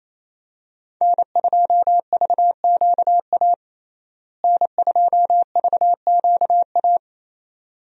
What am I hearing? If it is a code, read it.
Morse code, heard as a keyed tone, about 27 wpm, 700 Hz. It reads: N2VQA N2VQA
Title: USA Call Signs - 5 Characters - Repeated 2x - 28wpm